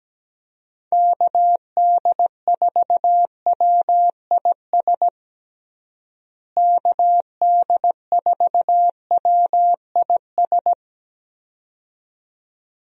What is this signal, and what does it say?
Morse code, heard as a keyed tone, about 17 wpm, 700 Hz. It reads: KD4WIS KD4WIS
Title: USA Call Signs - 6 Characters - Repeated 2x - 17wpm